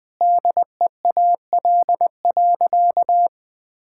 Morse code, heard as a keyed tone, about 20 wpm, 700 Hz. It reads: DEAL.
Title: Sentences with 3-Word Phrases - 20wpm